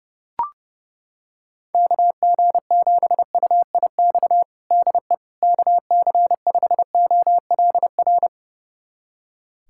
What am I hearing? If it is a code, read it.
Morse code, heard as a keyed tone, about 30 wpm, 700 Hz. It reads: KG7UIX DE KC5OLR
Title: USA Callsign DE Callsign - 6 Characters - 30wpm